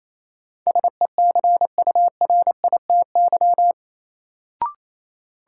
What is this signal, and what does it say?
Morse code, heard as a keyed tone, about 28 wpm, 700 Hz. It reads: SECURITY
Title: Sentences from Top 1400 Words - New Word Drill Alt - 28wpm